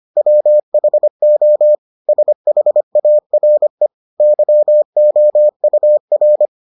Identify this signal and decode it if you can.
Morse code, heard as a keyed tone, about 25 wpm, 600 Hz. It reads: WHO SHARE YOUR